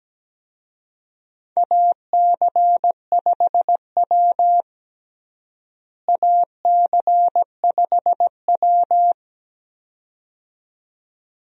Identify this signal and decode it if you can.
Morse code, heard as a keyed tone, about 17 wpm, 700 Hz. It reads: AC5W AC5W
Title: USA Call Signs - 4 Characters - Repeated 2x - 17wpm